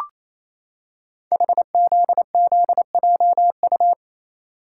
Morse code, heard as a keyed tone, about 27 wpm, 700 Hz. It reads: HZZJU
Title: Code Groups - 5 Letters - 28wpm